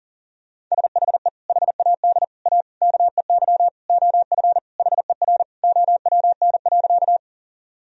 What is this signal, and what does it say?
Morse code, heard as a keyed tone, about 40 wpm, 700 Hz. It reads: SHE HAD A KEY OF HER OWN.
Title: Sentences from Top 1000 Words - Review Part 4 - 40wpm